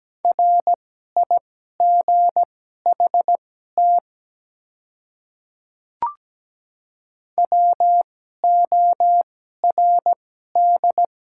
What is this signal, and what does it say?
Morse code, heard as a keyed tone, about 17 wpm, spaced as for 12 wpm, 700 Hz. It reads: RIGHT WORD
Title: 2x Character Spacing - Top 200 Words - 17wpm